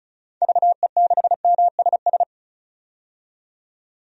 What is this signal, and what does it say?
Morse code, heard as a keyed tone, about 34 wpm, 700 Hz. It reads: VE6MSS
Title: DXCC-100 Call Signs - 6 Characters - 35wpm